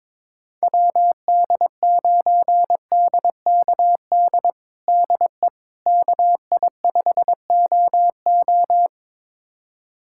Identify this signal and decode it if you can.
Morse code, heard as a keyed tone, about 21 wpm, 700 Hz. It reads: WD9DKD DE KI5OO
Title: USA Callsign DE Callsign 22wpm